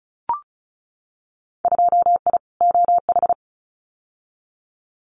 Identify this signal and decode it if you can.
Morse code, heard as a keyed tone, about 34 wpm, 700 Hz. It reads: 2I OH
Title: Winter Field Day Exchanges 35wpm